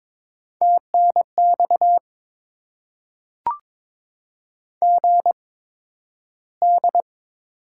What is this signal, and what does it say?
Morse code, heard as a keyed tone, about 22 wpm, spaced as for 12 wpm, 700 Hz. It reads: TNX GD
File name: Top 90 QSO Elements - 8x Character Spacing - 22wpm